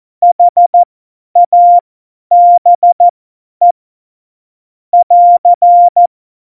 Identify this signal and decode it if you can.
Morse code, heard as a keyed tone, about 14 wpm, spaced as for 10 wpm, 700 Hz. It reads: HABE <AR>
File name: Morsekurs / Witze / Witz 14. Mai 2019, 50 F70